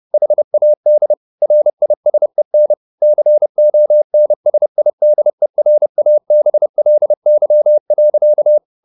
Morse code, heard as a keyed tone, about 30 wpm, 600 Hz. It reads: HAD RISEN CONSIDERABLY.